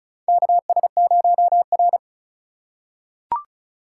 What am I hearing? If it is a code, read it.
Morse code, heard as a keyed tone, about 36 wpm, 700 Hz. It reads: KS0R
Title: USA Call Signs - 4 Characters - 3x Character Spacing - 35wpm